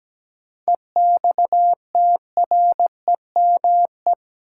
Morse code, heard as a keyed tone, about 17 wpm, 700 Hz. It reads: EXTREME
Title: Sentences from Top 1300 Words - New Word Drill Alt - 17wpm